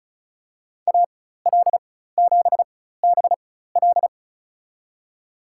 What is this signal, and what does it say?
Morse code, heard as a keyed tone, about 34 wpm, spaced as for 16 wpm, 700 Hz. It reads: AL7BL
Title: USA Call Signs - 5 Characters - 4x Character Spacing - 35wpm